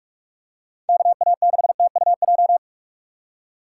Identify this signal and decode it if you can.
Morse code, heard as a keyed tone, about 45 wpm, 700 Hz. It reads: KA6TUJ
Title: USA Call Signs - 6 Characters - 4x Character Spacing - 45wpm